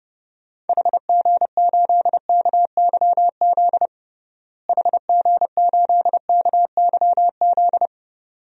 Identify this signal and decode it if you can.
Morse code, heard as a keyed tone, about 29 wpm, 700 Hz. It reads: HG8KYZ HG8KYZ
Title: DXCC-100 Call Signs - 6 Characters - Repeated 2x - 30wpm